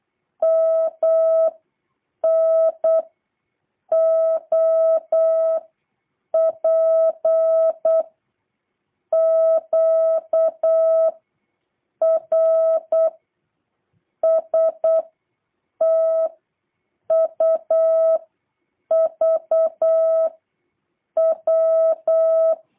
Morse code, heard as a keyed tone, about 8 wpm, 650 Hz. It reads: M N O P Q R S T U V W